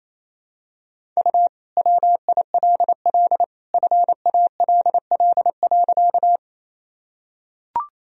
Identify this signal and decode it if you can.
Morse code, heard as a keyed tone, about 27 wpm, 700 Hz. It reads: U WILL FALL.
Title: Sentences from Top 500 Words and Abbreviations 28wpm